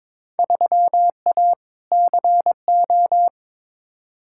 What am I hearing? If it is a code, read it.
Morse code, heard as a keyed tone, about 21 wpm, 700 Hz. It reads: 3A CO
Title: Field Day Exchanges Encore - 22wpm